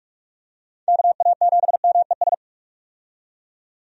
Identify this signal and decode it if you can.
Morse code, heard as a keyed tone, about 41 wpm, 700 Hz. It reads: KA7MES
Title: USA Call Signs - 6 Characters - 45wpm